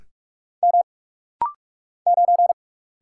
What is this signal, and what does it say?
Morse code, heard as a keyed tone, about 46 wpm, 700 Hz. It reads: M 9
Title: Single Letter-Number Alt - Rapid-Fire 45wpm